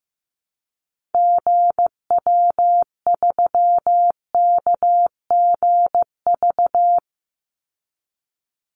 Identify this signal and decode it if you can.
Morse code, heard as a keyed tone, about 15 wpm, 700 Hz. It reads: GW3KGV
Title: Call Signs Repeated 3x - 15wpm